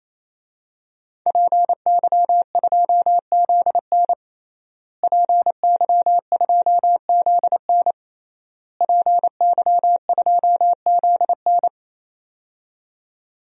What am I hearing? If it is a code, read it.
Morse code, heard as a keyed tone, about 27 wpm, 700 Hz. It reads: PY2ZN PY2ZN PY2ZN
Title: DXCC-100 Call Signs - 5 Characters - Repeated 3x - 28wpm